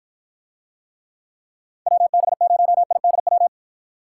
Morse code, heard as a keyed tone, about 52 wpm, 700 Hz. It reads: WB0IDW